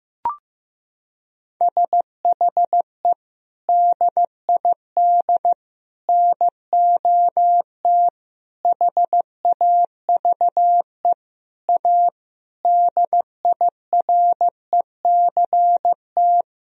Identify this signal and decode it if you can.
Morse code, heard as a keyed tone, about 15 wpm, 700 Hz. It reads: SHE DID NOT HAVE A DIRECT
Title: Sentences from Top 1000 Words - Review Part 2 - 15wpm